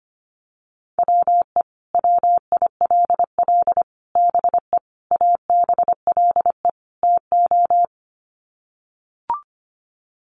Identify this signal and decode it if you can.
Morse code, heard as a keyed tone, about 25 wpm, 700 Hz. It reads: WE WILL BE ABLE TO
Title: Sets of 5 Words Encore 25wpm